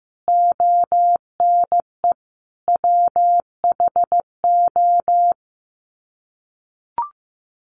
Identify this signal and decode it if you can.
Morse code, heard as a keyed tone, about 15 wpm, 700 Hz. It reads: ONE WHO
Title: Sets of 5 Words Encore2 15wpm